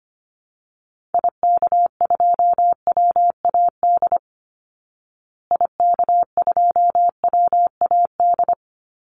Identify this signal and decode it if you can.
Morse code, heard as a keyed tone, about 25 wpm, 700 Hz. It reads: IK2WAD IK2WAD
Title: Call Signs Repeated 2x - 25wpm